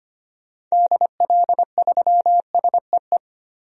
Morse code, heard as a keyed tone, about 25 wpm, 700 Hz. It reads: DL3SEE
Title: DXCC-100 Call Signs - 6 Characters - 25wpm